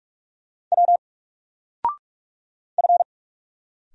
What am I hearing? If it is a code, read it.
Morse code, heard as a keyed tone, about 46 wpm, 700 Hz. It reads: W F
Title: Single Letter-Number 45wpm